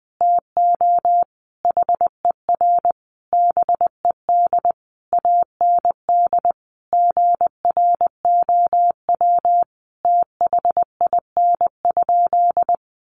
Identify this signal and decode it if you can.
Morse code, heard as a keyed tone, about 20 wpm, 700 Hz. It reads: TO HER BED AND GROW THIN?
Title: Sentences from Top 1200 Words 20wpm